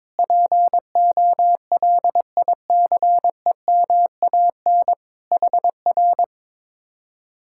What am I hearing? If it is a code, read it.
Morse code, heard as a keyed tone, about 22 wpm, 700 Hz. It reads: POLICEMAN HR
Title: Speed-Racing Common QSO Phrases Encore - 22wpm